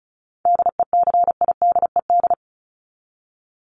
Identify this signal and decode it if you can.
Morse code, heard as a keyed tone, about 34 wpm, 700 Hz. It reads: DECIDED
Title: ED Suffix - Top 100 Words with Base Words 35wpm